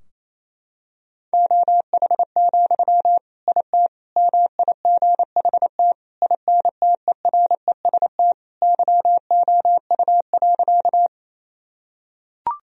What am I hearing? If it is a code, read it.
Morse code, heard as a keyed tone, about 28 wpm, 700 Hz. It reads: OH, IT MIGHT INTEREST YOU.
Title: Sentences from Top 1000 Words - Review Part 2 - 28wpm